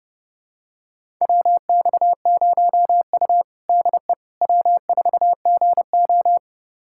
Morse code, heard as a keyed tone, about 30 wpm, 700 Hz. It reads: WX0U DE W4GO
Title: USA Callsign DE Callsign - 4 Characters - 30wpm